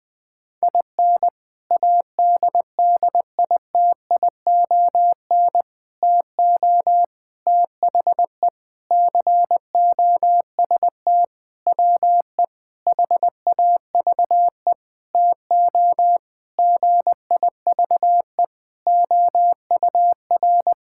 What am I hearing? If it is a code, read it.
Morse code, heard as a keyed tone, about 21 wpm, 700 Hz. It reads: IN ADDITION TO THE COST WE HAVE TO GIVE OUR
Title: Sentences with 4-Word Phrases - Spoken Beginning - 20wpm